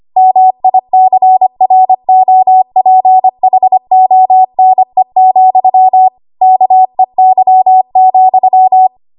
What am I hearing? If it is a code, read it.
Morse code, heard as a keyed tone, about 25 wpm, 750 Hz. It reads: MICROPHONE, KEY,
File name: Archive / 25 WPM / 25WPM